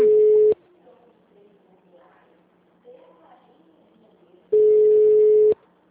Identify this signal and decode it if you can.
Ringback tone (Europe, 425 Hz), with speech.